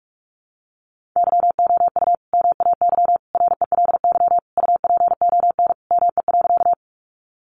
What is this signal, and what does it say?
Morse code, heard as a keyed tone, about 44 wpm, 700 Hz. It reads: YOU MAY RELY UPON ME.
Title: Sentences from Top 1200 Words 45wpm